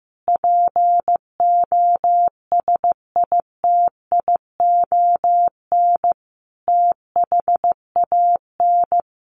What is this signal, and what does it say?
Morse code, heard as a keyed tone, about 15 wpm, 700 Hz. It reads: POSITION THAN
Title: Sentences from Top 1400 Words 15wpm